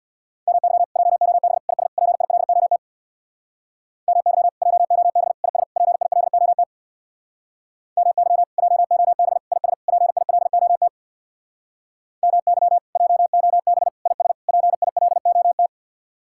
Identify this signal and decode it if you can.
Morse code, heard as a keyed tone, about 64 wpm, 700 Hz. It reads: MY JOB IS PILOT MY JOB IS PILOT MY JOB IS PILOT MY JOB IS PILOT